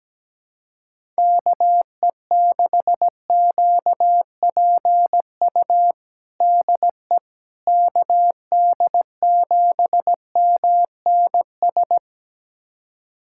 Morse code, heard as a keyed tone, about 17 wpm, 700 Hz. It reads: KE6QPU DE KD7MNS
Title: USA Callsign DE Callsign 17wpm